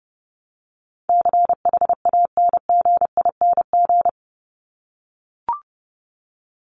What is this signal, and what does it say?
Morse code, heard as a keyed tone, about 30 wpm, 700 Hz. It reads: CHANGING
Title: ING Suffix - Top 100 Words with Base Words 30wpm